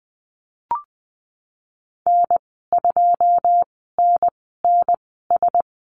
Morse code, heard as a keyed tone, about 20 wpm, spaced as for 15 wpm, 700 Hz. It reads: N2NNS
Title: 140 Lesson 23g - Letter f - Call Signs 20wpm